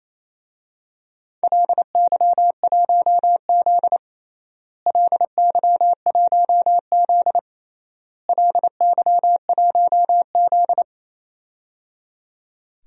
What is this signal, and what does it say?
Morse code, heard as a keyed tone, about 28 wpm, 700 Hz. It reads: LY1Z LY1Z LY1Z
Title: DXCC-100 Call Signs - 4 Characters - Repeated 3x - 28wpm